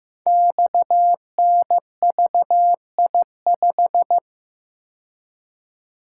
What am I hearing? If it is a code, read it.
Morse code, heard as a keyed tone, about 15 wpm, 700 Hz. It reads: XNVI5
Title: Code Groups - 5 Letter:Numbers - 15wpm